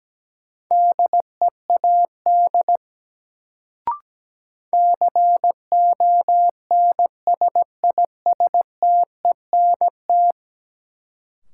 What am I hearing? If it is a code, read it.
Morse code, heard as a keyed tone, about 17 wpm, 700 Hz. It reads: DEAD CONSISTENT
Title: Sentences from Top 1500 Words - New Word Drill - 17wpm